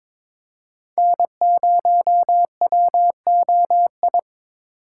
Morse code, heard as a keyed tone, about 22 wpm, 700 Hz. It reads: N0WOI
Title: K1USN SST QSO Practice - 22wpm